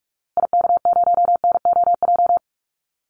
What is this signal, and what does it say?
Morse code, heard as a keyed tone, about 44 wpm, 700 Hz. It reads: IK0NOJ
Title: Call Signs Repeated 2x - 45wpm